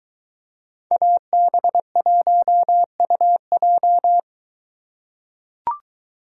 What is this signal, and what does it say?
Morse code, heard as a keyed tone, about 22 wpm, 700 Hz. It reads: AB1UJ